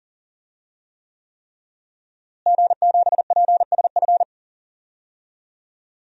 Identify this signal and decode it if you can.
Morse code, heard as a keyed tone, about 40 wpm, 700 Hz. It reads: G7PSF
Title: DXCC-100 Call Signs - 5 Characters - Repeated 3x - 40wpm